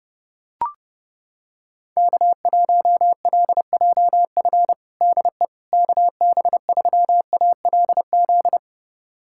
Morse code, heard as a keyed tone, about 30 wpm, 700 Hz. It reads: K1LJF DE KB3ALZ
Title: USA Callsign DE Callsign 30wpm